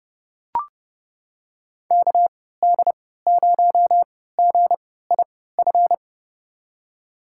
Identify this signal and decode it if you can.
Morse code, heard as a keyed tone, about 31 wpm, spaced as for 17 wpm, 700 Hz. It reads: KD0GIF
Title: USA Call Signs - 6 Characters - 3x Character Spacing - 30wpm